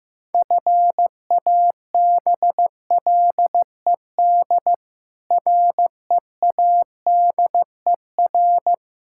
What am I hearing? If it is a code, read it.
Morse code, heard as a keyed tone, about 15 wpm, 700 Hz. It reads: FABLED READER
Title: Pangram Challenge v2 - 15wpm